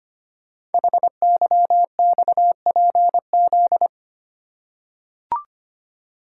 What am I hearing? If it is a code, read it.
Morse code, heard as a keyed tone, about 25 wpm, 700 Hz. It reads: HYXPZ
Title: Code Groups - 5 Letters - 25wpm